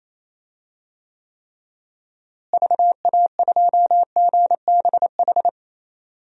Morse code, heard as a keyed tone, about 28 wpm, 700 Hz. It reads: VA2GBH
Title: DXCC-100 Call Signs - 6 Characters - Repeated 3x - 28wpm